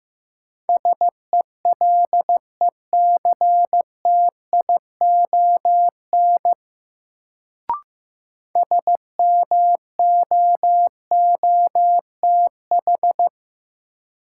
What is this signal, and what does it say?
Morse code, heard as a keyed tone, about 15 wpm, 700 Hz. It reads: SELECTION SMOOTH
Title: Sentences from Top 1400 Words - New Word Drill - 15wpm